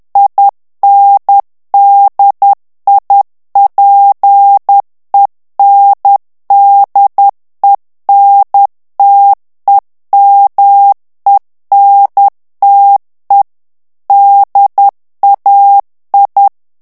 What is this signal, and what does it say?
Morse code, heard as a keyed tone, about 10 wpm, 800 Hz. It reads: INDIPENDENTEMENTE DAI
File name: ITALIEN texte clair 2 vitesse 10 mots minute